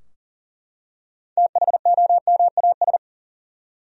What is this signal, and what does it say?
Morse code, heard as a keyed tone, about 40 wpm, 700 Hz. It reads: THOMAS
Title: USA Common Names - 40wpm